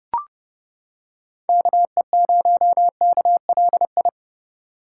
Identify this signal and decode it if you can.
Morse code, heard as a keyed tone, about 31 wpm, 700 Hz. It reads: KE0KLI